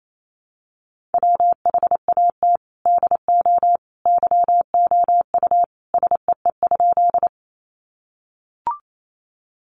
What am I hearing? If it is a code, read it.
Morse code, heard as a keyed tone, about 28 wpm, 700 Hz. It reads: WHAT DO YOU SEE?
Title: Sentences from Top 100 Words 28wpm